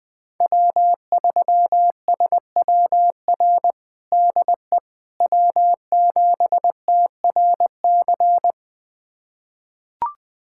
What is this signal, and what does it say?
Morse code, heard as a keyed tone, about 21 wpm, 700 Hz. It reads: W3SWR DE W7TRC
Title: USA Callsign DE Callsign - 5 Characters - 20wpm